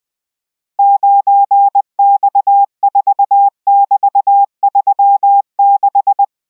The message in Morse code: 9X4<BT>36